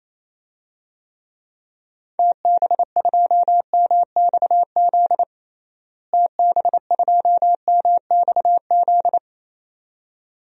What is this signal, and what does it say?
Morse code, heard as a keyed tone, about 27 wpm, 700 Hz. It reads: TB2MXZ TB2MXZ
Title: DXCC-100 Call Signs - 6 Characters - Repeated 2x - 28wpm